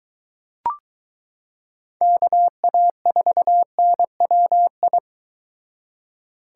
Morse code, heard as a keyed tone, about 22 wpm, 700 Hz. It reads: KA4NWI